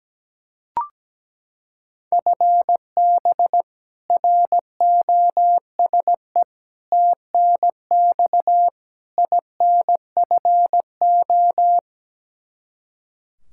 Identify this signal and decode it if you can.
Morse code, heard as a keyed tone, about 17 wpm, 700 Hz. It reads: FB ROSE TNX INFO
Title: Practice QSO - Exchange 3 - More Info - 17wpm